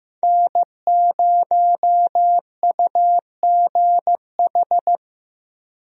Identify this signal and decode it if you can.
Morse code, heard as a keyed tone, about 15 wpm, 700 Hz. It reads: N0UGH
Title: USA Call Signs - 5 Characters - Repeated 3x - 15wpm